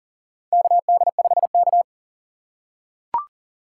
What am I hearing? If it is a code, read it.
Morse code, heard as a keyed tone, about 40 wpm, 700 Hz. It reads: KD5K
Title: USA Call Signs - 4 Characters - 4x Character Spacing - 40wpm